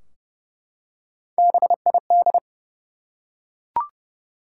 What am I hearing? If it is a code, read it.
Morse code, heard as a keyed tone, about 30 wpm, 700 Hz. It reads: BID
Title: Three Letter Words - 30wpm